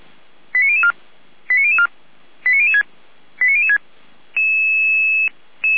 Five-tone selective call (ZVEI), heard as quick runs of five tones, tones of about 80 ms. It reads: 89004 89004 89006 89006